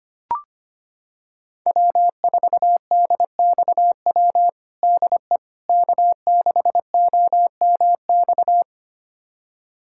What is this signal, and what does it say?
Morse code, heard as a keyed tone, about 25 wpm, 700 Hz. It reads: W4DXW DE K6OMX
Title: USA Callsign DE Callsign - 5 Characters - 25wpm